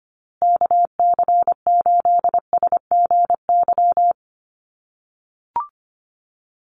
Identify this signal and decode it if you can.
Morse code, heard as a keyed tone, about 25 wpm, 700 Hz. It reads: KC8SGY
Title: USA Call Signs Repeated 2x 25wpm